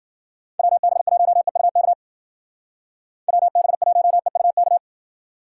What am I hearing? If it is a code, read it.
Morse code, heard as a keyed tone, about 49 wpm, 700 Hz. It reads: WB1EUK WB1EUK
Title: Speed Racing - USA 2x3 Call Signs 40wpm